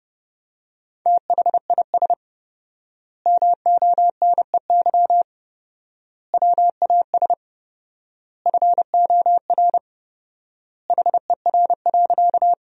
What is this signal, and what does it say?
Morse code, heard as a keyed tone, about 30 wpm, 700 Hz. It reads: THIS MONEY WAS FOR HER.
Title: 4x Word-Spacing - Sentences from Top 400 Words - 30wpm